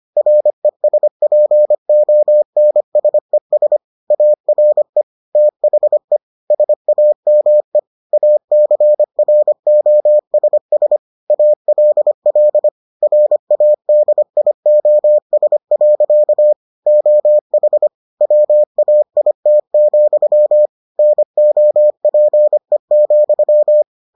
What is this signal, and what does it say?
Morse code, heard as a keyed tone, about 25 wpm, 600 Hz. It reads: RESPONSES ARE THE SAME ACROSS ALL RADIOS. OH WAIT, NOPE,